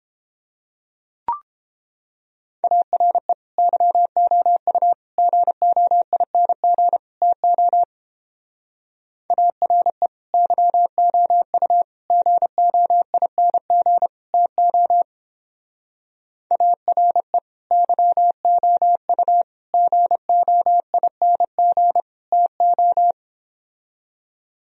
Morse code, heard as a keyed tone, about 29 wpm, 700 Hz. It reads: ARE YOU GOING TO ARE YOU GOING TO ARE YOU GOING TO